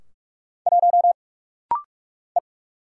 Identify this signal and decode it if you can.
Morse code, heard as a keyed tone, about 46 wpm, 700 Hz. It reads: 1 E
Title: Single Letter-Number Alt - Rapid-Fire 45wpm